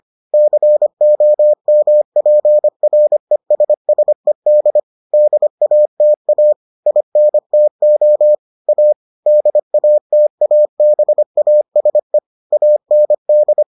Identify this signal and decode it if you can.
Morse code, heard as a keyed tone, about 25 wpm, 600 Hz. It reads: COMPRESSED DATA INTO A DATABASE AND